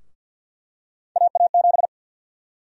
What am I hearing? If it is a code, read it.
Morse code, heard as a keyed tone, about 51 wpm, 700 Hz. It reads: AA7
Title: USA Call Sign Prefixes - 50wpm